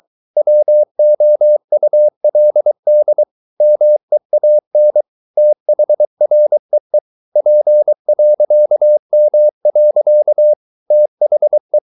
Morse code, heard as a keyed tone, about 23 wpm, 600 Hz. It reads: WOULD MEAN THREE P.M. THE